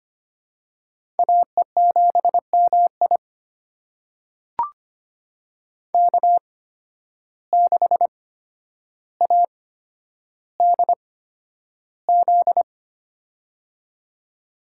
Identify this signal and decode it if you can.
Morse code, heard as a keyed tone, about 25 wpm, spaced as for 11 wpm, 700 Hz. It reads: AE7MI K6ADZ
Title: USA Call Signs - 5 Characters - 8x Character Spacing - 25wpm